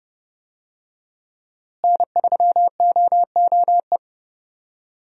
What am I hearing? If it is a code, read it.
Morse code, heard as a keyed tone, about 32 wpm, 700 Hz. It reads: N3OOE
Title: USA Call Signs - 5 Characters - Repeated 2x - 30wpm